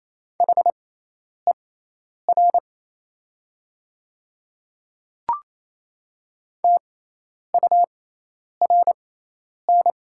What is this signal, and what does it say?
Morse code, heard as a keyed tone, about 27 wpm, spaced as for 10 wpm, 700 Hz. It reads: HER TURN
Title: 6x Character Spacing - Top 200 Words - 28wpm